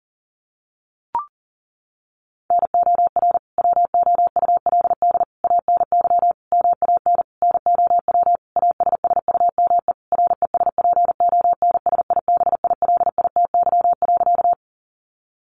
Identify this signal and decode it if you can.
Morse code, heard as a keyed tone, about 40 wpm, 700 Hz. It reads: NOR WOULD ANY MAN NOW ASSUME RESPONSIBILITY.
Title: Sentences from Top 1500 Words 40wpm